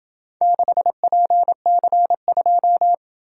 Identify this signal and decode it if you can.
Morse code, heard as a keyed tone, about 26 wpm, 700 Hz. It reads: 6PC2